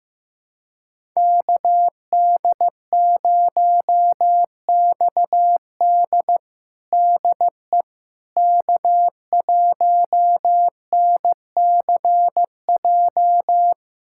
Morse code, heard as a keyed tone, about 15 wpm, 700 Hz. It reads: KD0XD DE K1NCJ
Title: USA Callsign DE Callsign - 5 Characters - 15wpm